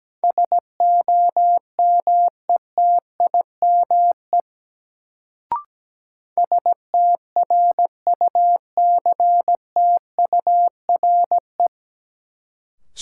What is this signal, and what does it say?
Morse code, heard as a keyed tone, about 17 wpm, 700 Hz. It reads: SOMETIME STRUCTURE
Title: Sentences from Top 1000 Words - Review Part 6 - New Word Drill - 17wpm